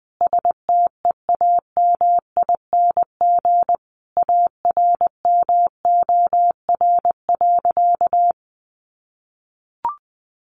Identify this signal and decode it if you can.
Morse code, heard as a keyed tone, about 20 wpm, 700 Hz. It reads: STEAMING ARMOR.
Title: A Sound Of Thunder 20wpm